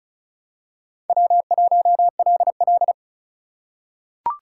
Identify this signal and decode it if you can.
Morse code, heard as a keyed tone, about 36 wpm, 700 Hz. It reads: W1LL
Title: USA Call Signs - 4 Characters - 3x Character Spacing - 35wpm